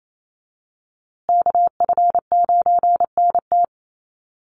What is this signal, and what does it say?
Morse code, heard as a keyed tone, about 28 wpm, 700 Hz. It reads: KF9NT
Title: USA Call Signs Repeated 2x 28wpm